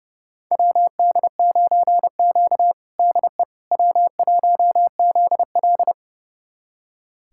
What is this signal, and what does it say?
Morse code, heard as a keyed tone, about 30 wpm, 700 Hz. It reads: WD9Q DE W1ZL
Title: USA Callsign DE Callsign - 4 Characters - 30wpm